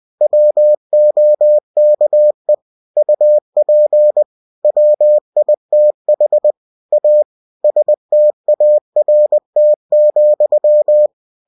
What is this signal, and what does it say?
Morse code, heard as a keyed tone, about 20 wpm, 600 Hz. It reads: WOKE UP WITH A START,
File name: cw-books / winnie-the-pooh / winnie-20wpm-page-0037